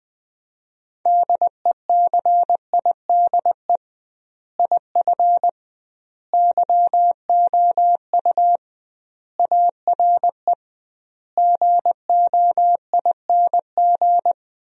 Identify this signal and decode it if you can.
Morse code, heard as a keyed tone, about 20 wpm, 700 Hz. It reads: DECIDE IF YOU ARE GOING
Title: 2x Word-Spacing - Sentences from Top 500 Words - 20wpm